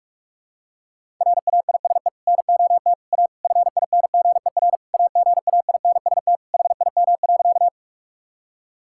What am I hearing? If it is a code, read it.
Morse code, heard as a keyed tone, about 41 wpm, 700 Hz. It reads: RAISE NOT A FINGER AGAINST HIM.
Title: Sentences from Top 1600 Words 45wpm